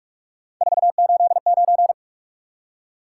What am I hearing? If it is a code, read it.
Morse code, heard as a keyed tone, about 44 wpm, 700 Hz. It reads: 489
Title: Top 180 QSO Elements 45wpm